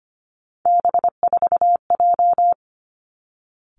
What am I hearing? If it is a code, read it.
Morse code, heard as a keyed tone, about 25 wpm, 700 Hz. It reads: B4J
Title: Three Challenging Letter-Number Combinations 25wpm